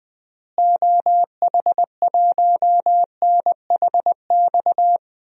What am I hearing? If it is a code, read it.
Morse code, heard as a keyed tone, about 21 wpm, 700 Hz. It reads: OH1NHX
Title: DXCC-100 Call Signs - 6 Characters - Repeated 3x - 20wpm